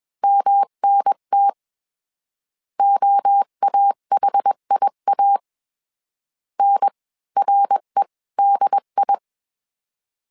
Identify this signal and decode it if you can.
Morse code, heard as a keyed tone, about 21 wpm, spaced as for 18 wpm, 800 Hz. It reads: MNT OAHIA NREDI
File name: CW / DET / 1 3 1500C S B 20wpm 1754